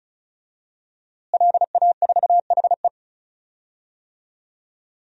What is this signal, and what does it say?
Morse code, heard as a keyed tone, about 36 wpm, 700 Hz. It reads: LA4HE
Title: DXCC-100 Call Signs - 5 Characters - Repeated 3x - 35wpm